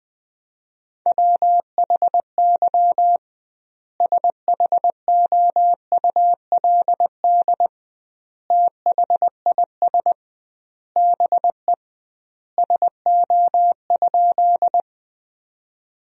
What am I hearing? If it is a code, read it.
Morse code, heard as a keyed tone, about 19 wpm, 700 Hz. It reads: WHY SHOULD THIS BE SO?
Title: 2x Word-Spacing - Sentences from Top 300 Words - 20wpm